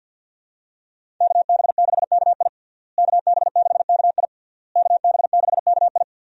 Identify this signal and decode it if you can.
Morse code, heard as a keyed tone, about 51 wpm, 700 Hz. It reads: KB6KI KB6KI KB6KI
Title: USA Call Signs - 5 Characters - Repeated 3x - 50wpm